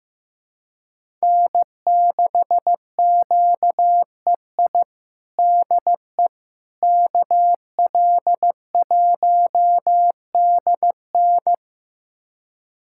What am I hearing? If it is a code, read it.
Morse code, heard as a keyed tone, about 15 wpm, 700 Hz. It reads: N6QEI DE KL1DN
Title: USA Callsign DE Callsign - 5 Characters - 15wpm